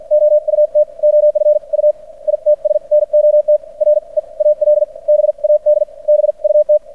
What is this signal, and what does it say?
Morse code, heard as a keyed tone, about 56 wpm, spaced as for 44 wpm, 600 Hz. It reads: OUT OF U ITS NOT REAL BAD BUT
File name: archived / CRSnet